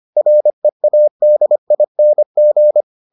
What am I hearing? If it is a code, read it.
Morse code, heard as a keyed tone, about 25 wpm, 600 Hz. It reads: READING